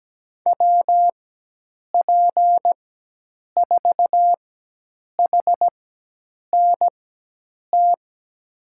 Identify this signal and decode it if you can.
Morse code, heard as a keyed tone, about 17 wpm, spaced as for 8 wpm, 700 Hz. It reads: WP4HNT
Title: USA Call Signs - 6 Characters - 4x Character Spacing - 17wpm